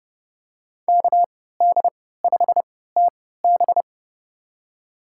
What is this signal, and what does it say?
Morse code, heard as a keyed tone, about 30 wpm, spaced as for 17 wpm, 700 Hz. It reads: KD5TB
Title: USA Call Signs - 5 Characters - 3x Character Spacing - 30wpm